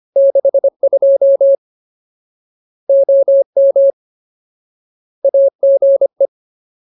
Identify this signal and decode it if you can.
Morse code, heard as a keyed tone, about 25 wpm, 550 Hz. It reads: 62 OM AGE
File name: Ultimate Word List / 25WPM Ultimate